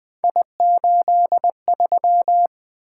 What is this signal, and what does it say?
Morse code, heard as a keyed tone, about 20 wpm, 700 Hz. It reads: I83
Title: Three Letter-Number Combinations - 20wpm